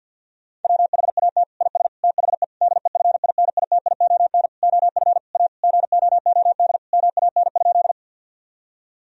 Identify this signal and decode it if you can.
Morse code, heard as a keyed tone, about 50 wpm, 700 Hz. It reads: WHAT IS THE DEFINITION OF A GOOD MAN?
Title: Sentences from Top 1000 Words - Review Part 4 - 50wpm